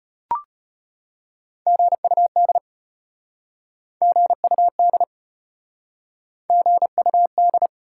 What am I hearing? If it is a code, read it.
Morse code, heard as a keyed tone, about 33 wpm, 700 Hz. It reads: GUD GUD GUD